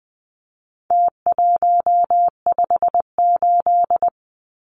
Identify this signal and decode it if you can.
Morse code, heard as a keyed tone, about 20 wpm, 700 Hz. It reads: T158
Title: Numbers - Contest Style - 4-Digit 20wpm